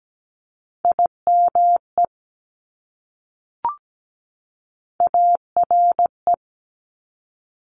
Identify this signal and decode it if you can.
Morse code, heard as a keyed tone, about 17 wpm, 700 Hz. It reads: IME ARE
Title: Top Three Letter Combinations in Words 17wpm